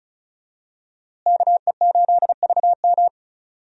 Morse code, heard as a keyed tone, about 34 wpm, 700 Hz. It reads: KE8VM
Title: USA Call Signs - 5 Characters - Repeated 3x - 35wpm